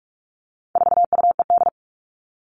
Morse code, heard as a keyed tone, about 44 wpm, 700 Hz. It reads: 4UED
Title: All Cut Numbers - 4-Digit 45wpm